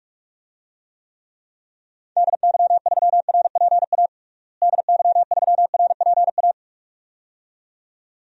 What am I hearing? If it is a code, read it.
Morse code, heard as a keyed tone, about 44 wpm, 700 Hz. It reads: DY3RPA DY3RPA
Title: DXCC-100 Call Signs - 6 Characters - Repeated 2x - 45wpm